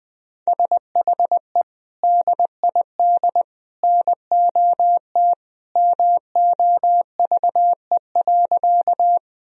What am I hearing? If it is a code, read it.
Morse code, heard as a keyed tone, about 20 wpm, 700 Hz. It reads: SHE DID NOT MOVE.
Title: Sentences from Top 1000 Words - Review Part 1 - 20wpm